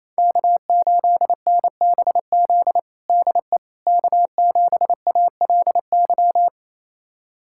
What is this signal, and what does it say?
Morse code, heard as a keyed tone, about 28 wpm, 700 Hz. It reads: K8NBZ DE K7ALY
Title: USA Callsign DE Callsign - 5 Characters - 28wpm